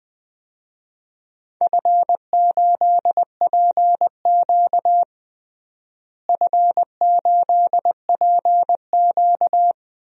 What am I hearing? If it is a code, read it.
Morse code, heard as a keyed tone, about 20 wpm, 700 Hz. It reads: F8PQ F8PQ
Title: DXCC-100 Call Signs - 4 Characters - Repeated 3x - 20wpm